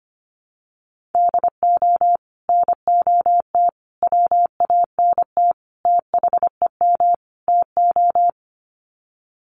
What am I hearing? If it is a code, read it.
Morse code, heard as a keyed tone, about 25 wpm, 690 Hz. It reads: DO NOT WANT THEM TO
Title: Sets of 5 Words Encore 25wpm